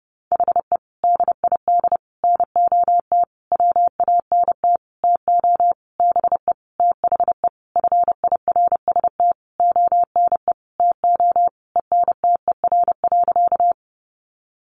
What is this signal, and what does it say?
Morse code, heard as a keyed tone, about 30 wpm, 700 Hz. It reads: HE DID NOT WANT TO BE THE FIRST ONE TO ENTER.
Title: Sentences from Top 1300 Words 30wpm